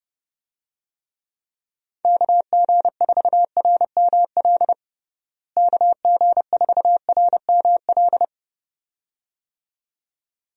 Morse code, heard as a keyed tone, about 32 wpm, 700 Hz. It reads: KG4RML KG4RML
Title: USA Call Signs - 6 Characters - Repeated 2x - 30wpm